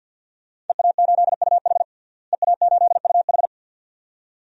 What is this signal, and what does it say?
Morse code, heard as a keyed tone, about 53 wpm, 700 Hz. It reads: EA8UH EA8UH
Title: DXCC-100 Call Signs - 5 Characters - Repeated 2x - 50wpm